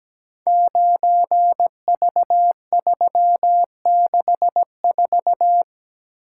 Morse code, heard as a keyed tone, about 17 wpm, 700 Hz. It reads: 9V364
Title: Code Groups - 5 Letter:Numbers - 17wpm